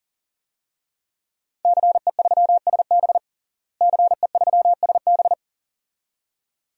Morse code, heard as a keyed tone, about 40 wpm, 700 Hz. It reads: CE3SB CE3SB
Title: DXCC-100 Call Signs - 5 Characters - Repeated 2x - 40wpm